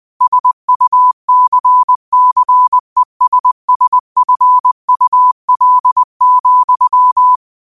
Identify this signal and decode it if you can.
Morse code, heard as a keyed tone, about 20 wpm, 1000 Hz. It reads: SUCCESSFUL,
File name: qotdfiles / 20wpm